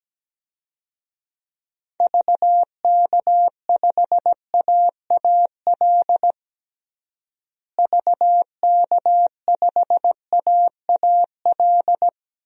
Morse code, heard as a keyed tone, about 17 wpm, 700 Hz. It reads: VK5AAL VK5AAL
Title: DXCC-100 Call Signs - 6 Characters - Repeated 2x - 17wpm